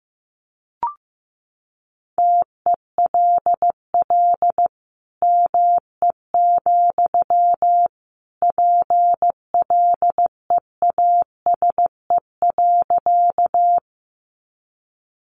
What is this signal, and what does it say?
Morse code, heard as a keyed tone, about 15 wpm, 700 Hz. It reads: TELL ME, PLEASE.
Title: Sentences from Top 1500 Words 15wpm